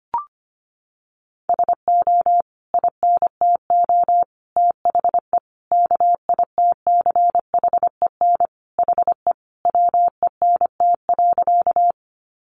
Morse code, heard as a keyed tone, about 25 wpm, 700 Hz. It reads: SO INTO THE KITCHEN HE WENT.
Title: Sentences from Top 1400 Words 25wpm